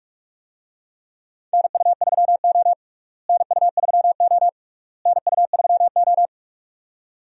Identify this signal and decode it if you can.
Morse code, heard as a keyed tone, about 46 wpm, 700 Hz. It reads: NU3O NU3O NU3O
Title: USA Call Signs - 4 Characters - Repeated 3x - 45wpm